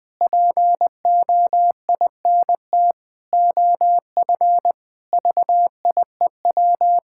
Morse code, heard as a keyed tone, about 20 wpm, 700 Hz. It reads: POINT OF VIEW
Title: Sentences from Top 1000 Words - Review Part 6 - 20wpm